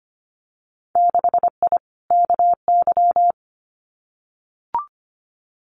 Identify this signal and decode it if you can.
Morse code, heard as a keyed tone, about 25 wpm, 700 Hz. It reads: 6I KY
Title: Winter Field Day Exchanges 25wpm